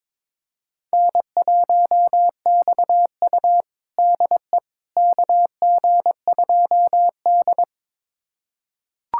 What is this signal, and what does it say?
Morse code, heard as a keyed tone, about 21 wpm, 700 Hz. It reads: N1XU DE KG2D
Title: USA Callsign DE Callsign - 4 Characters - 22wpm